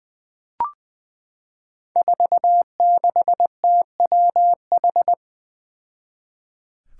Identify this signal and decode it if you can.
Morse code, heard as a keyed tone, about 20 wpm, 700 Hz. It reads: 46TWH
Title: Five Challenging Letter-Number Combinations 20wpm